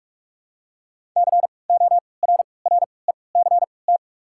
Morse code, heard as a keyed tone, about 46 wpm, spaced as for 25 wpm, 700 Hz. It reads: CORRECT
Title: 3x Character Spacing - Top 500 Words - 45wpm